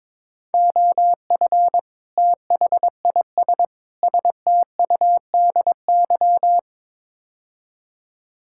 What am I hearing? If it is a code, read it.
Morse code, heard as a keyed tone, about 22 wpm, 700 Hz. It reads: OF THIS STUDY
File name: practice / 22wpm / Build A Phrase - 5-Word Phrases - 22wpm